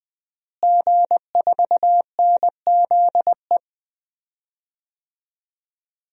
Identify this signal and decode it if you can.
Morse code, heard as a keyed tone, about 20 wpm, 700 Hz. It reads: G4NZE
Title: DXCC-100 Call Signs - 5 Characters - Repeated 2x - 20wpm